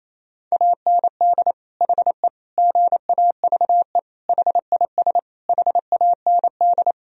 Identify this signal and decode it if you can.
Morse code, heard as a keyed tone, about 28 wpm, 700 Hz. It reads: AND HE GAVE HIS HAND
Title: Sentences from Top 1000 Words - Review Part 4 - 28wpm